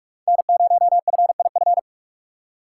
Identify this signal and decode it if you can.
Morse code, heard as a keyed tone, about 42 wpm, 700 Hz. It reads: N0FIF